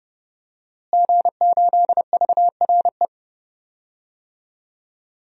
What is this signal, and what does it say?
Morse code, heard as a keyed tone, about 30 wpm, 700 Hz. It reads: G8VRE
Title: DXCC-100 Call Signs - 5 Characters - Repeated 3x - 30wpm